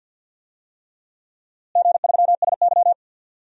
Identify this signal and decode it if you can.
Morse code, heard as a keyed tone, about 50 wpm, 700 Hz. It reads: G3SY
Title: DXCC-100 Call Signs - 4 Characters - Repeated 3x - 50wpm